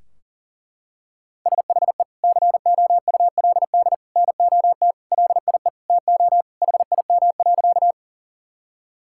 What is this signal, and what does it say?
Morse code, heard as a keyed tone, about 40 wpm, 700 Hz. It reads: SHE COULD NOT LIE TO HIM.
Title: Sentences from Top 1600 Words 40wpm